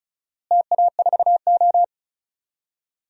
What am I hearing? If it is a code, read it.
Morse code, heard as a keyed tone, about 36 wpm, 700 Hz. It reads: TA4O
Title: DXCC-100 Call Signs - 4 Characters - Repeated 3x - 35wpm